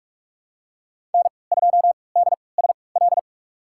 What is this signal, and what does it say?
Morse code, heard as a keyed tone, about 46 wpm, spaced as for 25 wpm, 700 Hz. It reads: N2DSL
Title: USA Call Signs - 5 Characters - 3x Character Spacing - 45wpm